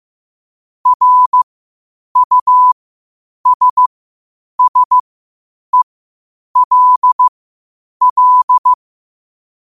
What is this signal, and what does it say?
Morse code, heard as a keyed tone, about 15 wpm, spaced as for 8 wpm, 1000 Hz. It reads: RUSSELL
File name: qotdfiles / 8wpm